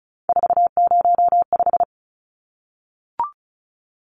Morse code, heard as a keyed tone, about 34 wpm, 700 Hz. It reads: 405
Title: Numbers - 3-Digit 35wpm